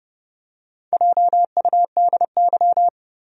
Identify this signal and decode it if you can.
Morse code, heard as a keyed tone, about 30 wpm, 700 Hz. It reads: JUDY
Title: USA Common Names - Max 5 Letters - 30wpm